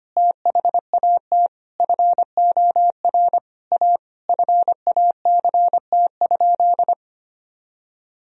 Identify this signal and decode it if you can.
Morse code, heard as a keyed tone, about 25 wpm, 700 Hz. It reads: THAT FOR A FACT?
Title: Sentences from Top 1000 Words - Review Part 3 - 25wpm